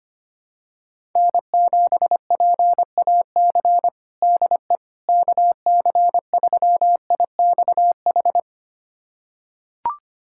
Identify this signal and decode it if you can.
Morse code, heard as a keyed tone, about 25 wpm, 700 Hz. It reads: N7PAC DE KC3IXH
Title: USA Callsign DE Callsign 25wpm